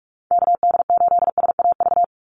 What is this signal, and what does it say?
Morse code, heard as a keyed tone, about 44 wpm, 700 Hz. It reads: KD8SAV
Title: Call Signs Repeated 2x - 45wpm